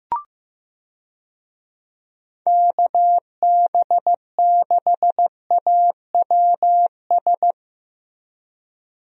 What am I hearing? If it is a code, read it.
Morse code, heard as a keyed tone, about 15 wpm, 700 Hz. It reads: KB6AWS
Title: USA Call Signs - 6 Characters - Repeated 3x - 15wpm